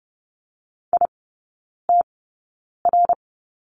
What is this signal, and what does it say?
Morse code, heard as a keyed tone, about 30 wpm, spaced as for 9 wpm, 700 Hz. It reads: ITR
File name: Lesson 17e - Letter u - Three Characters 30wpm